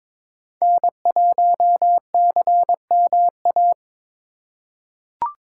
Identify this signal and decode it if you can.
Morse code, heard as a keyed tone, about 22 wpm, 700 Hz. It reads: N1CMA
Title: USA Call Signs - 3x Character Spacing - 22wpm